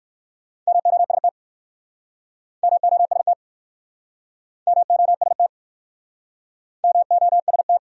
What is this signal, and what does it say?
Morse code, heard as a keyed tone, about 55 wpm, 700 Hz. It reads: MOST MOST MOST MOST
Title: Speed-Racing Top 200 Words 45wpm